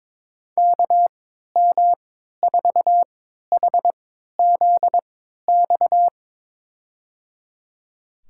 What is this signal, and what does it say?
Morse code, heard as a keyed tone, about 21 wpm, spaced as for 12 wpm, 700 Hz. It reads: KM4HZX
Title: USA Call Signs - 6 Characters - 3x Character Spacing - 22wpm